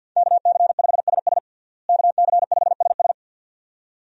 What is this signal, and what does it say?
Morse code, heard as a keyed tone, about 52 wpm, 700 Hz. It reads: KC5SS KC5SS
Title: USA Call Signs - 5 Characters - Repeated 2x - 50wpm